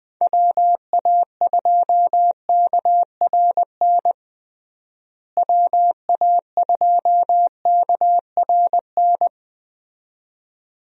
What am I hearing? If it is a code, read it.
Morse code, heard as a keyed tone, about 21 wpm, 700 Hz. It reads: WA2KRN WA2KRN
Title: USA Call Signs - 6 Characters - Repeated 2x - 20wpm